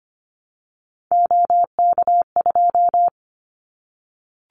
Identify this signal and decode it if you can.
Morse code, heard as a keyed tone, about 25 wpm, 700 Hz. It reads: OK2
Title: Callsign Prefixes for Top 100 DXCC - 25wpm